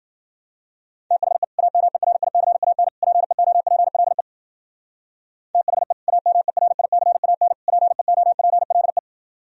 Morse code, heard as a keyed tone, about 51 wpm, 700 Hz. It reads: THE AMERICAN PEOPLE THE AMERICAN PEOPLE